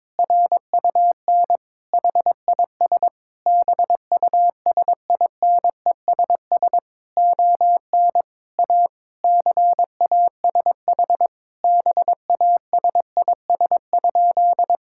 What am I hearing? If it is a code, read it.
Morse code, heard as a keyed tone, about 22 wpm, 700 Hz. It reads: RUN HIS BUSINESS ON A CASH BASIS?
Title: Sentences from Top 1000 Words - Review Part 5 - 22wpm